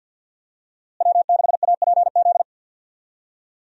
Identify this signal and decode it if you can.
Morse code, heard as a keyed tone, about 52 wpm, 700 Hz. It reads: W6APZ
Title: USA Call Signs - 5 Characters - Repeated 3x - 50wpm